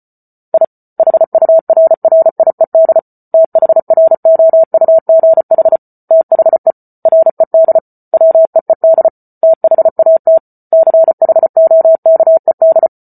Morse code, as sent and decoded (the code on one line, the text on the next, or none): .. / .... ..- .-. .-. .. -.. / - .... .-. --- ..- --. .... / - .... / .-. -.. / .-- . -.. / - .... .- - / -.-. .... --- -.- . -..
I HURRIED THROUGH THE RED WEED THAT CHOKED